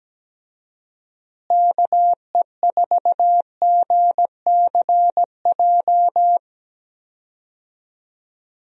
Morse code, heard as a keyed tone, about 17 wpm, 700 Hz. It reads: KE4GCJ
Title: USA Call Signs - 6 Characters - Repeated 3x - 17wpm